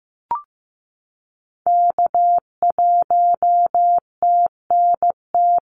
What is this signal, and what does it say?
Morse code, heard as a keyed tone, about 15 wpm, 700 Hz. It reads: K1TNT
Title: NA SOTA Activator Callsigns 15wpm